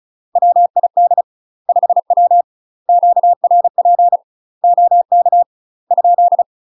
Morse code, heard as a keyed tone, about 35 wpm, 700 Hz. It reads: WID 5W QRP OK ?
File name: UK MASTER QSOs TRAINING / 35WPM / UK MASTER QSOs 0034